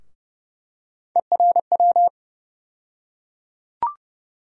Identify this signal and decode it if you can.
Morse code, heard as a keyed tone, about 30 wpm, 700 Hz. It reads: ERW
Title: USA Call Sign Postfixes - 3 Letters 30wpm